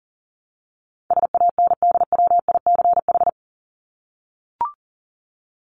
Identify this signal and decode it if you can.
Morse code, heard as a keyed tone, about 40 wpm, 700 Hz. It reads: SANDWICH
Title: Top 1751-2000 Words 40wpm